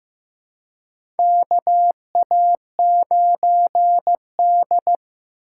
Morse code, heard as a keyed tone, about 15 wpm, 700 Hz. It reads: KA9D
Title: USA Call Signs - 4 Characters - 3x Character Spacing - 15wpm